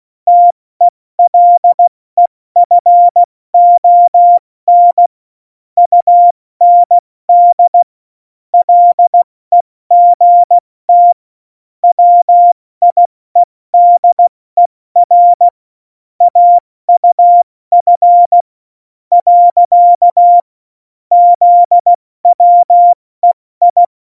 Morse code, heard as a keyed tone, about 16 wpm, spaced as for 14 wpm, 700 Hz. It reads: TELEFON UND LEGT WIEDER AUF. ZWEI
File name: Morsekurs / Witze / Witz 02. Oktober 2018, 70 F80